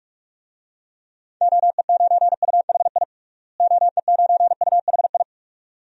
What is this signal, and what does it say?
Morse code, heard as a keyed tone, about 46 wpm, 700 Hz. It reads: OE9UHI OE9UHI
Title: DXCC-100 Call Signs - 6 Characters - Repeated 2x - 45wpm